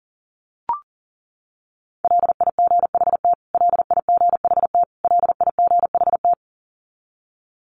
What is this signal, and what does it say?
Morse code, heard as a keyed tone, about 40 wpm, 700 Hz. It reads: LIGHT LIGHT LIGHT
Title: Top 200 Words Repeated 40wpm